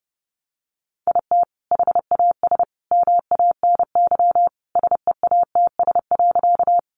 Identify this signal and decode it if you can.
Morse code, heard as a keyed tone, about 30 wpm, 700 Hz. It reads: IT HAS MANY SEATS.
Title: Additional Easy Phrases 30wpm